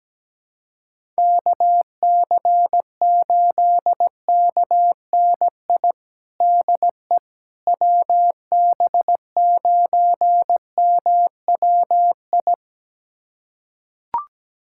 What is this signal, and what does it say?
Morse code, heard as a keyed tone, about 17 wpm, 700 Hz. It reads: KC8KNI DE WB9MWI
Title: USA Callsign DE Callsign 17wpm